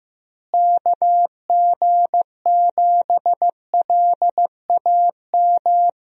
Morse code, heard as a keyed tone, about 15 wpm, 700 Hz. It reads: KG7LAM
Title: USA Call Signs - 6 Characters - Repeated 3x - 15wpm